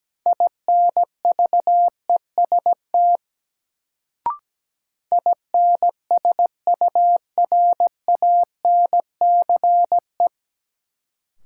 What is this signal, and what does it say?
Morse code, heard as a keyed tone, about 17 wpm, 700 Hz. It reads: INVEST INSURANCE
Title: Sentences from Top 1100 Words - New Word Drill - 17wpm